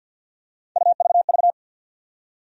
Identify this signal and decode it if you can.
Morse code, heard as a keyed tone, about 52 wpm, 700 Hz. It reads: UVV
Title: USA Call Sign Postfixes - 3 Letters 50wpm